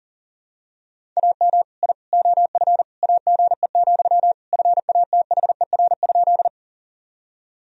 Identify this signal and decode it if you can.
Morse code, heard as a keyed tone, about 37 wpm, 700 Hz. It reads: AM I OF AGE, FATHER?
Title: Sentences from Top 1000 Words - Review Part 3 - 40wpm